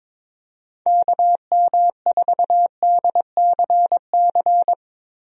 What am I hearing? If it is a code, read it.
Morse code, heard as a keyed tone, about 22 wpm, 700 Hz. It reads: KM4DCC
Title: USA Call Signs - 6 Characters - Repeated 3x - 22wpm